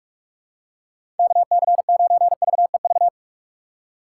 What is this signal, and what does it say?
Morse code, heard as a keyed tone, about 44 wpm, 700 Hz. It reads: KC9VEV
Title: USA Call Signs - 6 Characters - 6x Character Spacing - 45wpm